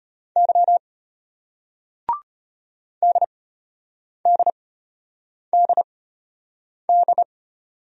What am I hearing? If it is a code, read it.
Morse code, heard as a keyed tone, about 35 wpm, spaced as for 8 wpm, 700 Hz. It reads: Y DDDD